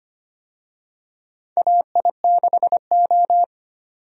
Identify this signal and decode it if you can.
Morse code, heard as a keyed tone, about 25 wpm, 700 Hz. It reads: AI6O
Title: USA Call Signs - 4 Characters - 25wpm